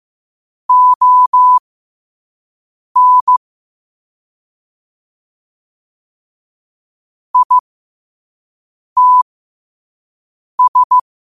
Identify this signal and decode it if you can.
Morse code, heard as a keyed tone, about 15 wpm, spaced as for 5 wpm, 1000 Hz. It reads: ON ITS
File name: qotdfiles / 5wpm